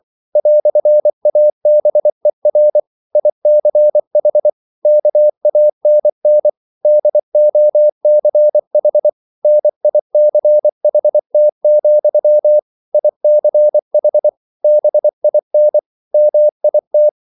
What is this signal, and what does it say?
Morse code, heard as a keyed tone, about 24 wpm, 600 Hz. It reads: "ABER ICH KANN DOCH NICHT, ICH BIN MIT